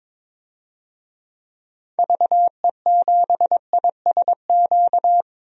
Morse code, heard as a keyed tone, about 22 wpm, 700 Hz. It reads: VE7ISQ
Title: DXCC-100 Call Signs - 6 Characters - 22wpm